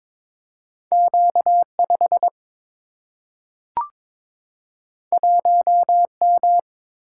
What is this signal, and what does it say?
Morse code, heard as a keyed tone, about 22 wpm, 700 Hz. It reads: Q 5 1 M
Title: Single-Letter Single-Number 22wpm